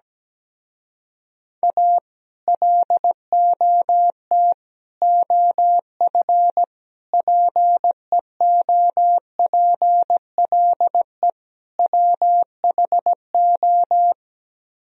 Morse code, heard as a keyed tone, about 17 wpm, 700 Hz. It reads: A LOT OF PEOPLE WHO
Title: Speed-Racing - Sets of 5 Words 17wpm